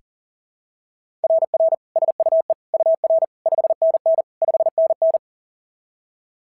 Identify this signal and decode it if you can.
Morse code, heard as a keyed tone, about 40 wpm, 690 Hz. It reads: RR SUE UR 5NN 5NN
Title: SOTA QSO Practice 40wpm